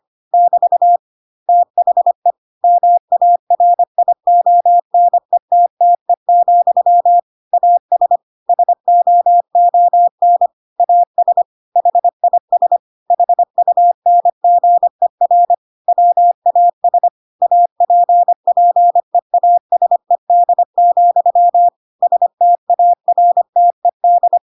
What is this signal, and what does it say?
Morse code, heard as a keyed tone, about 25 wpm, 700 Hz. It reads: <BT> THE MARIONETTE, AS SOON AS HIS HUNGER WAS APPEASED, STARTED
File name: Pinocchio / Pinocchio ch8 25 25 ik1hge